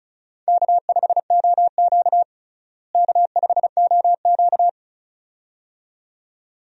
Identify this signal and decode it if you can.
Morse code, heard as a keyed tone, about 36 wpm, 700 Hz. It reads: K5OQ K5OQ
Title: USA Call Signs - 4 Characters - Repeated 2x - 35wpm